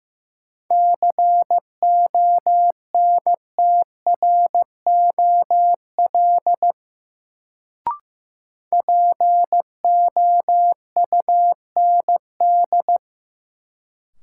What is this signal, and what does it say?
Morse code, heard as a keyed tone, about 16 wpm, 700 Hz. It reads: CONTROL POUND
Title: Sentences from Top 1000 Words - Review Part 2 - New Word Drill - 15wpm